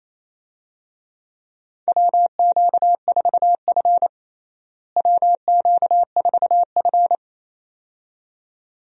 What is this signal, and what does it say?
Morse code, heard as a keyed tone, about 28 wpm, 700 Hz. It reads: WQ4F WQ4F
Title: USA Call Signs - 4 Characters - Repeated 2x - 28wpm